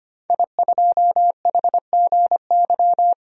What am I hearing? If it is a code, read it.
Morse code, heard as a keyed tone, about 25 wpm, 700 Hz. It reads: I2HGY
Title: DXCC-100 Call Signs - 5 Characters - 25wpm